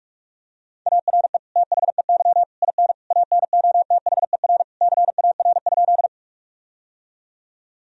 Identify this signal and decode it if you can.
Morse code, heard as a keyed tone, about 46 wpm, 700 Hz. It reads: ARE THEY IN ANOTHER CAR?
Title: Sentences from Top 1000 Words - Review Part 1 - 45wpm